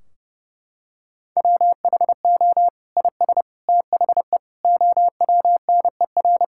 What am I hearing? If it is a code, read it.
Morse code, heard as a keyed tone, about 30 wpm, 700 Hz. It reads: WHO IS THE OWNER
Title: Sentences from Top 1600 Words 30wpm